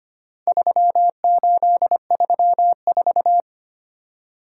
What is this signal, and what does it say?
Morse code, heard as a keyed tone, about 25 wpm, 700 Hz. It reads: 3834
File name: practice / 25wpm / Four Letter-Number Combinations - 25wpm